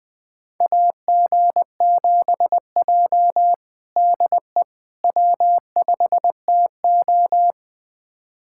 Morse code, heard as a keyed tone, about 20 wpm, 700 Hz. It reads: AG7J DE W5TO
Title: USA Callsign DE Callsign - 4 Characters - 20wpm